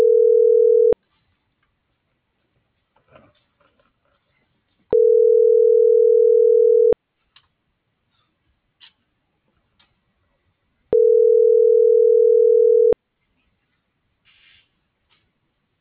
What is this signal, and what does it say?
Ringback tone (North America), 440 + 480 Hz.